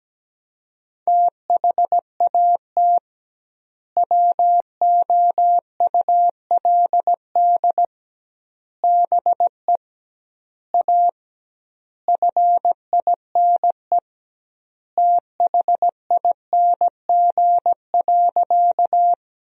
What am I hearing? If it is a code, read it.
Morse code, heard as a keyed tone, about 17 wpm, 700 Hz. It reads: THAT WOULD BE A FINE THING.
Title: 2x Word-Spacing - Sentences from Top 500 Words - 17wpm